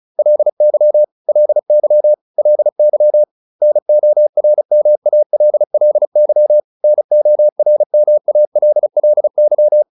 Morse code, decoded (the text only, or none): LY LY LY NORMALLY NORMALLY